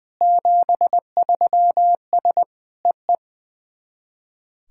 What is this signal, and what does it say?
Morse code, heard as a keyed tone, about 20 wpm, 690 Hz. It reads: 73S EE
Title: SOTA QSO Practice 20wpm